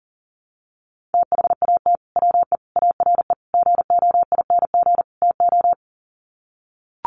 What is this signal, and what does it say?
Morse code, heard as a keyed tone, about 40 wpm, 700 Hz. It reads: THAT WE ARE GOING TO
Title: Sets of 5 Words Encore 40wpm